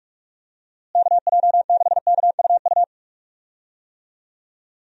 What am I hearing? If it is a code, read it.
Morse code, heard as a keyed tone, about 46 wpm, 700 Hz. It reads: KJ6KUU
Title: USA Call Signs - 6 Characters - Repeated 3x - 45wpm